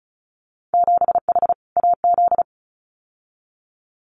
Morse code, heard as a keyed tone, about 34 wpm, 700 Hz. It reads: 7H AZ
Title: Winter Field Day Exchanges Encore 35wpm